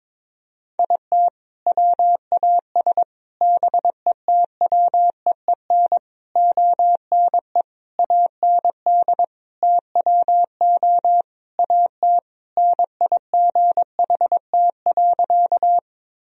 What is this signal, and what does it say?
Morse code, heard as a keyed tone, about 22 wpm, 700 Hz. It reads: IT WAS BETWEEN ONE AND TWO AT NIGHT.
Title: Sentences from Top 1000 Words - Review Part 1 - 22wpm